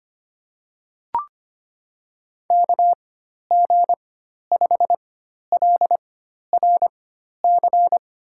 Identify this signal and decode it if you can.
Morse code, heard as a keyed tone, about 25 wpm, spaced as for 12 wpm, 700 Hz. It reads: KG5LRC
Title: USA Call Signs - 6 Characters - 4x Character Spacing - 25wpm